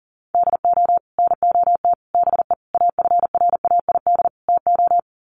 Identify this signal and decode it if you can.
Morse code, heard as a keyed tone, about 40 wpm, 700 Hz. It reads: DO NOT BE AFRAID TO
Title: Sets of 5 Words Encore2 40wpm